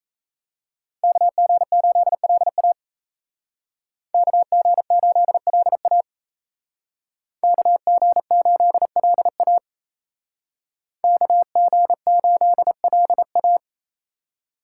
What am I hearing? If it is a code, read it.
Morse code, heard as a keyed tone, about 33 wpm, 700 Hz. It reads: KG8LA KG8LA KG8LA KG8LA
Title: Speed Racing - USA Call Signs 28wpm